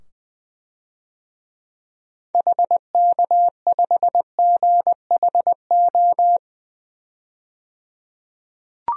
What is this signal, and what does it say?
Morse code, heard as a keyed tone, about 20 wpm, 700 Hz. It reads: HK5GHO
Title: DXCC-100 Call Signs - 6 Characters - 20wpm